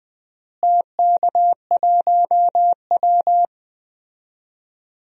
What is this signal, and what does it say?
Morse code, heard as a keyed tone, about 20 wpm, 700 Hz. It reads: TK1W
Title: DXCC-100 Call Signs - 4 Characters - Repeated 2x - 20wpm